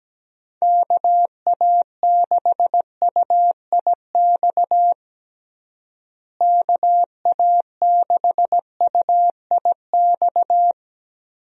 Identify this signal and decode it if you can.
Morse code, heard as a keyed tone, about 17 wpm, 700 Hz. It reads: KA6UIX KA6UIX
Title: USA Call Signs - 6 Characters - Repeated 2x - 17wpm